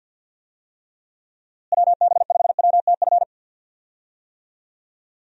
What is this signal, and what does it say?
Morse code, heard as a keyed tone, about 48 wpm, 700 Hz. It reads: WB5WTF
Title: USA Call Signs - 6 Characters - Repeated 2x - 50wpm